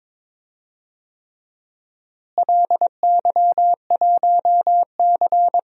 Morse code, heard as a keyed tone, about 22 wpm, 700 Hz. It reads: LY1C
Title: DXCC-100 Call Signs - 4 Characters - Repeated 3x - 22wpm